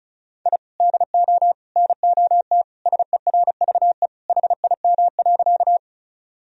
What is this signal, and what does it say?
Morse code, heard as a keyed tone, about 36 wpm, 700 Hz. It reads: I DO NOT SERVE HIM.
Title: Sentences from Top 1000 Words - Review Part 2 - 35wpm